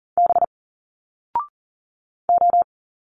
Morse code, heard as a keyed tone, about 40 wpm, 700 Hz. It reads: B O
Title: Single Letters 40wpm